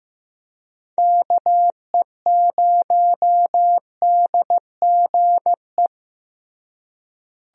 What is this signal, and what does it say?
Morse code, heard as a keyed tone, about 15 wpm, 700 Hz. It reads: KE0DGE
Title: USA Call Signs - 6 Characters - Repeated 2x - 15wpm